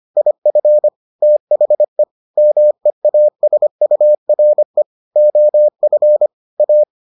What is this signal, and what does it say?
Morse code, heard as a keyed tone, about 25 wpm, 600 Hz. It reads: IF THE MEASURE OF A